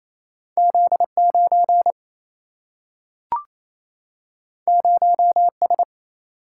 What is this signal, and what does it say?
Morse code, heard as a keyed tone, about 28 wpm, 700 Hz. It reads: Z9 0S
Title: Single-Letter Single-Number 28wpm